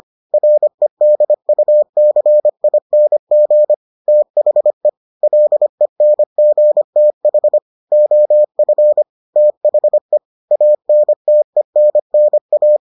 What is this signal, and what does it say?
Morse code, heard as a keyed tone, about 25 wpm, 600 Hz. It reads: REDUCING THE LENGTH OF THE ANTENNA